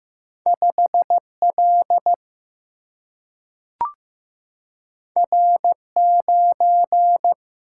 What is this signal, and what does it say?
Morse code, heard as a keyed tone, about 15 wpm, 700 Hz. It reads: 5L R9
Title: Single-Letter Single-Number 15wpm